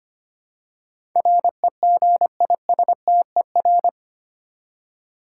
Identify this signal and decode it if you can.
Morse code, heard as a keyed tone, about 25 wpm, 700 Hz. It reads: REGISTER
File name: 8 Letter Words - First 3 Letters Given - 25wpm